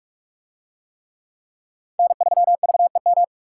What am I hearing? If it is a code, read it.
Morse code, heard as a keyed tone, about 42 wpm, 700 Hz. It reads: N3VEM